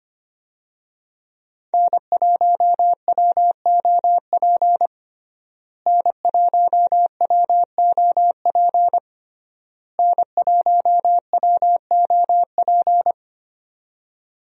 Morse code, heard as a keyed tone, about 25 wpm, 700 Hz. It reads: N1WOP N1WOP N1WOP
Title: USA Call Signs - 5 Characters - Repeated 3x - 25wpm